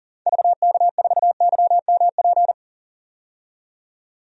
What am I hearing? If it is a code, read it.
Morse code, heard as a keyed tone, about 40 wpm, 700 Hz. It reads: VK4YMP
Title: DXCC-100 Call Signs - 6 Characters - 40wpm